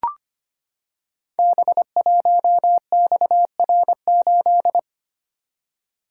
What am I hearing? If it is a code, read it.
Morse code, heard as a keyed tone, about 25 wpm, 690 Hz. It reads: B1XR8
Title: Code Groups - 5 Letter:Numbers - 25wpm